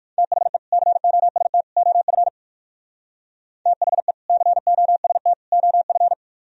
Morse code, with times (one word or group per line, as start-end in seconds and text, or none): 0.2-0.6 THE
0.7-1.6 COST
1.8-2.3 OF
3.7-4.1 THE
4.3-5.3 COST
5.5-6.1 OF